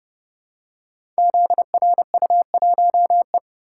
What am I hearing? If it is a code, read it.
Morse code, heard as a keyed tone, about 30 wpm, 700 Hz. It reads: ZRU1E
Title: Code Groups - 5 Letter:Numbers - 30wpm